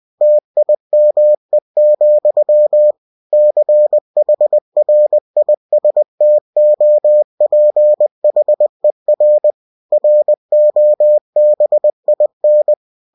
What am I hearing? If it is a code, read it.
Morse code, heard as a keyed tone, about 20 wpm, 600 Hz. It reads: TIME, CHRISTOPHER ROBIN
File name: cw-books / winnie-the-pooh / winnie-20wpm-page-0020